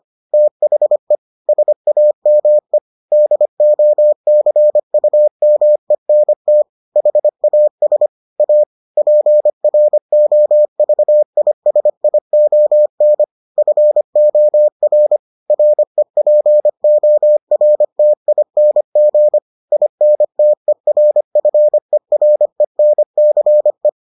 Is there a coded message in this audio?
Yes: Morse code from 0.3 to 23.9 s, heard as a keyed tone, about 25 wpm, 600 Hz. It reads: THE SAME DOCUMENT HAS A PROVISION FOR REPORTING INTERFERENCE